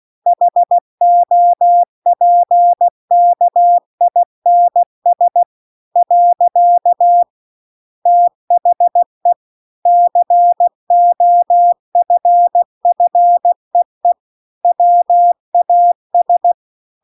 Morse code, heard as a keyed tone, about 16 wpm, 700 Hz. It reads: HOPKINS . THE COFFEE WAS